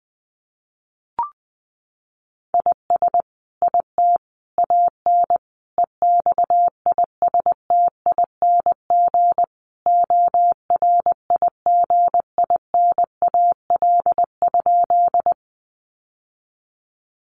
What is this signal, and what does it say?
Morse code, heard as a keyed tone, about 20 wpm, 700 Hz. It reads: IS IT AN EXISTING ORIGINAL?
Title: Sentences from Top 1500 Words 20wpm